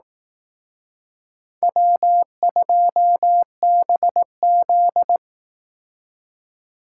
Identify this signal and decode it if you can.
Morse code, heard as a keyed tone, about 18 wpm, 700 Hz. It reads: W2BZ
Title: Speed Racing - USA 1x2 Call Signs 15wpm